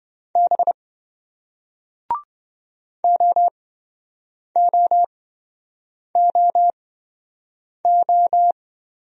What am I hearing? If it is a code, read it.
Morse code, heard as a keyed tone, about 27 wpm, spaced as for 7 wpm, 700 Hz. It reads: B OOOO